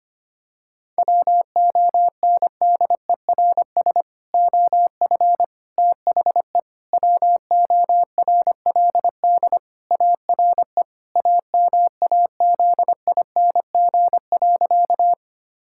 Morse code, heard as a keyed tone, about 25 wpm, 700 Hz. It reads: WONDERS OF THE WORLD ARE AMAZING.
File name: practice / 25wpm / Sentences with 3-Word Phrases - 25wpm